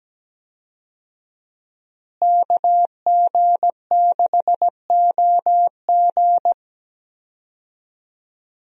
Morse code, heard as a keyed tone, about 17 wpm, 700 Hz. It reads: KG6OG
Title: USA Call Signs - 5 Characters - Repeated 2x - 17wpm